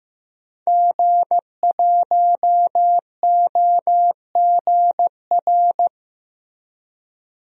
Morse code, heard as a keyed tone, about 15 wpm, 700 Hz. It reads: G1OGR
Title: DXCC-100 Call Signs - 5 Characters - Repeated 2x - 15wpm